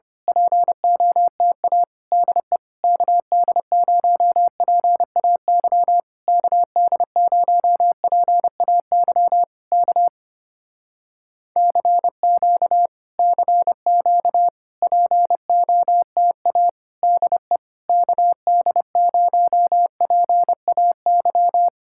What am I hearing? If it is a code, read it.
Morse code, heard as a keyed tone, about 25 wpm, 700 Hz. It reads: POTA DE KD0PAY KD0PAY K CQ CQ POTA DE KD0PAY
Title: Speed-Racing Common QSO Phrases Encore - 25wpm